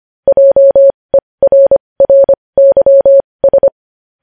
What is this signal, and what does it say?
Morse code, heard as a keyed tone, about 25 wpm, spaced as for 20 wpm, 550 Hz. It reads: JERRYS
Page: news-headlines-in-morse-code-at-20-wpm